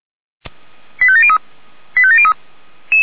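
Five-tone selective call (ZVEI), heard as quick runs of five tones, tones of about 70 ms. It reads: 85703 85703
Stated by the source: ZVEI five-tone selective call received by SDR.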